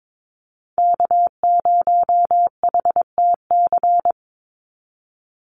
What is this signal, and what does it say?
Morse code, heard as a keyed tone, about 22 wpm, 700 Hz. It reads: K0HTC
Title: USA Call Signs 22wpm